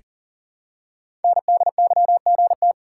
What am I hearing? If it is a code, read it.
Morse code, heard as a keyed tone, about 38 wpm, 700 Hz. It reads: NDYGT
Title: Code Groups - 5 Letter:Numbers - 40wpm